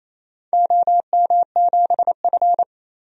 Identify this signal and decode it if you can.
Morse code, heard as a keyed tone, about 28 wpm, 700 Hz. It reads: OM7F
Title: DXCC-100 Call Signs - 4 Characters - 28wpm